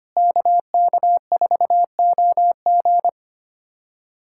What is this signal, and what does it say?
Morse code, heard as a keyed tone, about 25 wpm, 700 Hz. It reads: KK4OG
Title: USA Callsign DE Callsign - 5 Characters - 25wpm